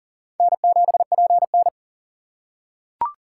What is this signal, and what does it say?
Morse code, heard as a keyed tone, about 40 wpm, 700 Hz. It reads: N7PN
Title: USA Call Signs - 4 Characters - 8x Character Spacing - 40wpm